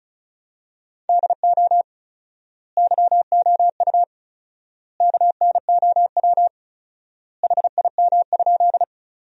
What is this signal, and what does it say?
Morse code, heard as a keyed tone, about 34 wpm, 700 Hz. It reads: DO YOU KNOW HIM?
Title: 4x Word-Spacing - Sentences from Top 100 Words - 35wpm